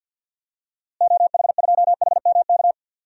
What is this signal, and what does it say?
Morse code, heard as a keyed tone, about 46 wpm, 700 Hz. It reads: OH2HMK
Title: DXCC-100 Call Signs - 6 Characters - Repeated 3x - 50wpm